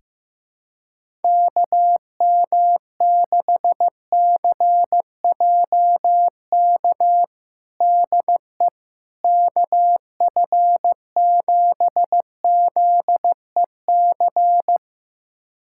Morse code, heard as a keyed tone, about 15 wpm, 700 Hz. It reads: KM6CJK DE KF7ZEC
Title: USA Callsign DE Callsign - 6 Characters - 15wpm